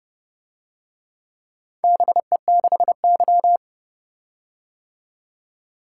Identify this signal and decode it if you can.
Morse code, heard as a keyed tone, about 29 wpm, 700 Hz. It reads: BE6Y